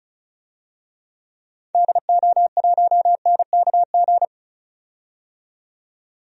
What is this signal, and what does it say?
Morse code, heard as a keyed tone, about 36 wpm, 700 Hz. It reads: DO1NKG
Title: DXCC-100 Call Signs - 6 Characters - 35wpm